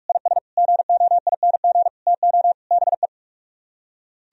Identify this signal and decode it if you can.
Morse code, heard as a keyed tone, about 46 wpm, 700 Hz. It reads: IS GOING TO BE